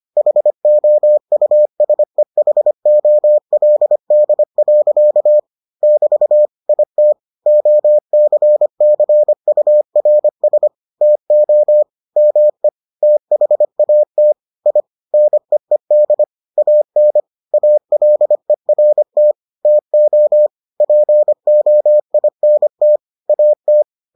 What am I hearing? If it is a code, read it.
Morse code, heard as a keyed tone, about 25 wpm, 600 Hz. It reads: HOUSEHOLD. <BT> IT OCCURS TO ME THAT I NEED AN ALERT TO POINT AT